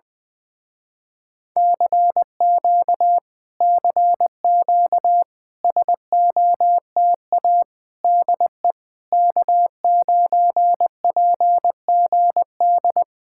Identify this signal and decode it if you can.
Morse code, heard as a keyed tone, about 20 wpm, 700 Hz. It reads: CQ CQ SOTA DE K9PGD